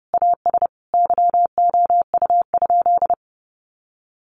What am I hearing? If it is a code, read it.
Morse code, heard as a keyed tone, about 30 wpm, 700 Hz. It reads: AS YOU?
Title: Sentences from Top 1000 Words 30wpm